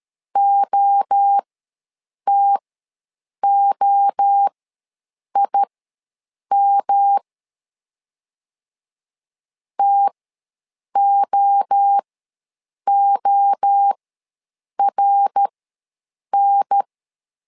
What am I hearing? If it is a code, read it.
Morse code, heard as a keyed tone, about 13 wpm, spaced as for 7 wpm, 800 Hz. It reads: OTOIM TOORN
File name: LEZIONI / 3 MWRO / 3 MWRO ws 12wpm 1914